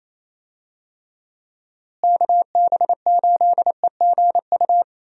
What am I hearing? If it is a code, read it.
Morse code, heard as a keyed tone, about 28 wpm, 700 Hz. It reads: KB8EGU
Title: USA Call Signs - 6 Characters - Repeated 3x - 28wpm